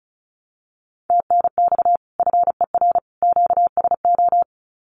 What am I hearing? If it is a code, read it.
Morse code, heard as a keyed tone, about 34 wpm, 700 Hz. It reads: TNX FER QSO
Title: Common QSO Phrases 35wpm